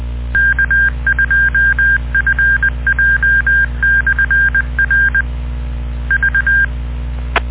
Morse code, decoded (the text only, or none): K2FJ/R V